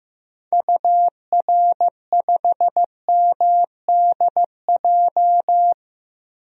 Morse code, heard as a keyed tone, about 16 wpm, 700 Hz. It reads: UR5MDJ
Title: DXCC-100 Call Signs - 6 Characters - Repeated 3x - 15wpm